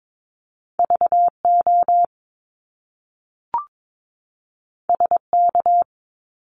Morse code, heard as a keyed tone, about 22 wpm, 700 Hz. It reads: VO SK
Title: Top Two Letter Combinations in Words 22wpm